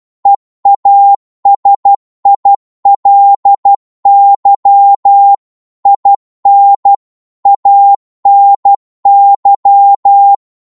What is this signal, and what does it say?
Morse code, heard as a keyed tone, about 12 wpm, 800 Hz. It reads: EASILY IN ANY